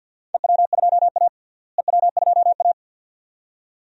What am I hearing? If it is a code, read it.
Morse code, heard as a keyed tone, about 50 wpm, 700 Hz. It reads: EW2A EW2A
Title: DXCC-100 Call Signs - 4 Characters - Repeated 2x - 50wpm